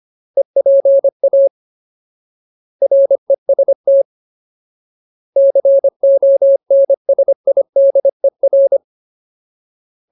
Morse code, heard as a keyed tone, about 25 wpm, 550 Hz. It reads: EPA REST CONSIDER
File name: Ultimate Word List / 25WPM Ultimate